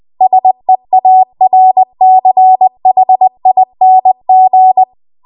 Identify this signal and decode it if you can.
Morse code, heard as a keyed tone, about 20 wpm, 750 Hz. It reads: SEARCHING